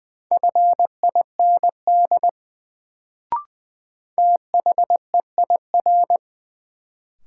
Morse code, heard as a keyed tone, about 20 wpm, 700 Hz. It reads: FIND THEIR
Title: Sets of 2 Words - New Word Drill - 20wpm